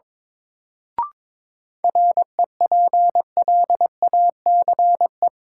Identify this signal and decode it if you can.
Morse code, heard as a keyed tone, about 21 wpm, 700 Hz. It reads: REPLACE
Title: Sentences from Top 1400 Words - New Word Drill - 22wpm